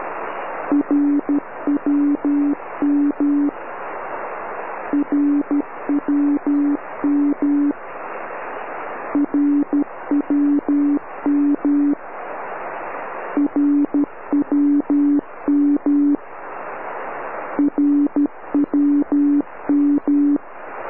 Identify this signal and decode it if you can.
Morse code, heard as a keyed tone, about 12 wpm, 290 Hz. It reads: RWM RWM RWM RWM RWM
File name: RWM cut